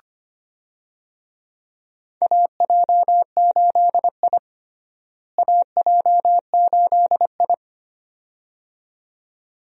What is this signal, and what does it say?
Morse code, heard as a keyed tone, about 25 wpm, 700 Hz. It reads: AJ8I AJ8I
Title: USA Call Signs - 4 Characters - Repeated 2x - 25wpm